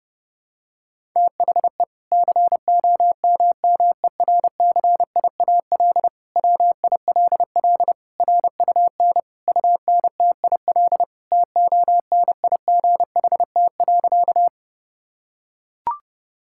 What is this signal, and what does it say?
Morse code, heard as a keyed tone, about 30 wpm, 700 Hz. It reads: THE COMMERCIAL WILL RUN UNTIL TONIGHT.
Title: Sentences from Top 1000 Words - Review Part 5 - 30wpm